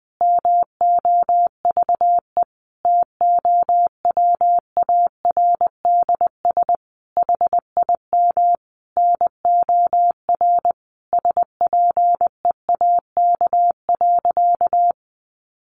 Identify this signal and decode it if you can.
Morse code, heard as a keyed tone, about 20 wpm, 700 Hz. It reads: MOVE TOWARDS HIM NOR SPEAK.
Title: Sentences from Top 1500 Words 20wpm